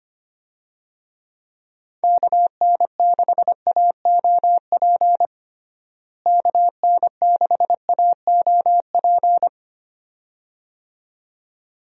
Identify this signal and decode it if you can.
Morse code, heard as a keyed tone, about 25 wpm, 700 Hz. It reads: KN6AOP KN6AOP
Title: USA Call Signs - 6 Characters - Repeated 2x - 25wpm